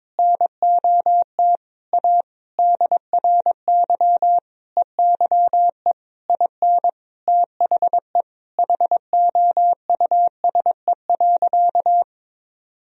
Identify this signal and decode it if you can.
Morse code, heard as a keyed tone, about 22 wpm, 700 Hz. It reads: NOT A DRY EYE IN THE HOUSE.
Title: Sentences from Top 1000 Words - Review Part 3 - 22wpm